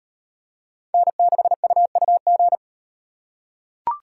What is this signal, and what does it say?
Morse code, heard as a keyed tone, about 37 wpm, 700 Hz. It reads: N6UUG